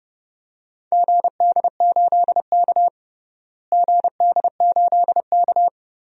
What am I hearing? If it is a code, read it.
Morse code, heard as a keyed tone, about 31 wpm, 700 Hz. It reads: GD8K GD8K
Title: DXCC-100 Call Signs - 4 Characters - Repeated 3x - 30wpm